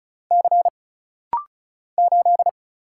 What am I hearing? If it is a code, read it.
Morse code, heard as a keyed tone, about 34 wpm, 700 Hz. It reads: C 8
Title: Single Letter-Number Alt - Rapid-Fire 35wpm